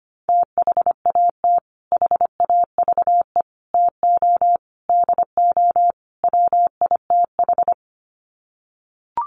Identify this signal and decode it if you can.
Morse code, heard as a keyed tone, about 25 wpm, 700 Hz. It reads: THAT HAVE TO DO WITH
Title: Sets of 5 Words Encore2 25wpm